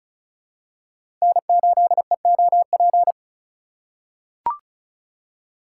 Morse code, heard as a keyed tone, about 34 wpm, 700 Hz. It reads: N8EOP
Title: USA Call Signs - 5 Characters - 4x Character Spacing - 35wpm